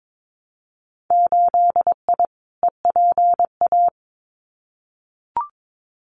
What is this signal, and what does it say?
Morse code, heard as a keyed tone, about 22 wpm, 700 Hz. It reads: 8I EPA
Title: Winter Field Day Exchanges Encore 22wpm